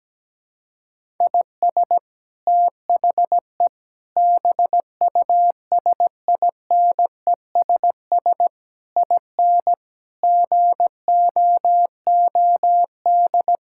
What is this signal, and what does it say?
Morse code, heard as a keyed tone, about 17 wpm, 700 Hz. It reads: IS THE BUSINESS IN GOOD
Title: Sentences from Top 1000 Words - Review Part 5 - 17wpm